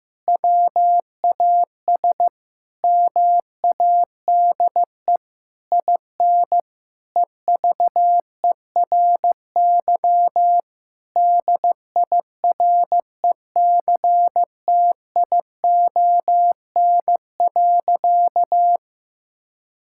Morse code, heard as a keyed tone, about 15 wpm, 700 Hz. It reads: WAS MADE IN EVERY DIRECTION.
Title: Sentences from Top 1000 Words - Review Part 5 - 15wpm